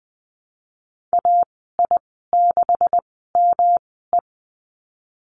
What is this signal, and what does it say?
Morse code, heard as a keyed tone, about 20 wpm, spaced as for 15 wpm, 700 Hz. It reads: AI6ME
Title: 128 Lesson 21g - Number 6 - Call Signs 20wpm